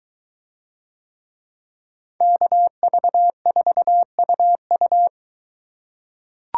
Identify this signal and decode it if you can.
Morse code, heard as a keyed tone, about 22 wpm, 700 Hz. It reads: KV4UU